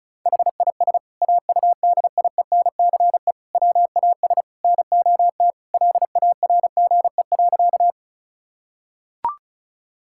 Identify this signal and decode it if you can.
Morse code, heard as a keyed tone, about 36 wpm, 700 Hz. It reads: HIS AUDIENCE WAS NOT LARGE.
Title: Sentences from Top 1000 Words - Review Part 5 - 35wpm